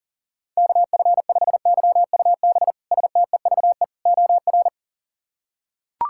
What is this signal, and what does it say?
Morse code, heard as a keyed tone, about 40 wpm, 700 Hz. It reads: KF5YUB STEVE OR
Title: K1USN SST QSO Practice - 40wpm